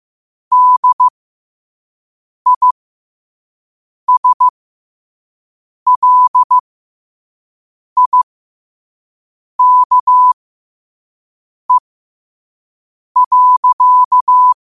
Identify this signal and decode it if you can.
Morse code, heard as a keyed tone, about 15 wpm, spaced as for 5 wpm, 1000 Hz. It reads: DISLIKE.
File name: qotdfiles / 5wpm